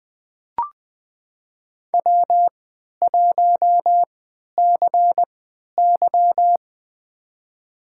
Morse code, heard as a keyed tone, about 20 wpm, spaced as for 11 wpm, 700 Hz. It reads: W1CY
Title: USA Call Signs - 4 Characters - 3x Character Spacing - 20wpm